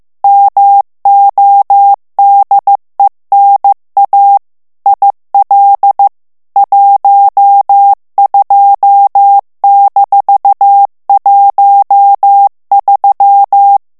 Morse code, heard as a keyed tone, about 15 wpm, 800 Hz. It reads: MODENA IL 12-13
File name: ITALIEN texte clair 1 vitesse 14 mots minute